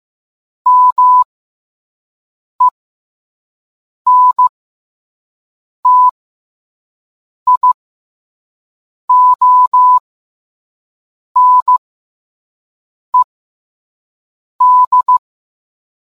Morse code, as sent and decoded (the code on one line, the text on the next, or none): -- . -. - .. --- -. -..
MENTIONED